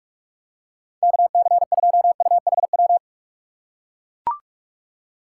Morse code, heard as a keyed tone, about 45 wpm, 700 Hz. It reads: KC2UHW
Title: USA Call Signs - 6 Characters - 4x Character Spacing - 45wpm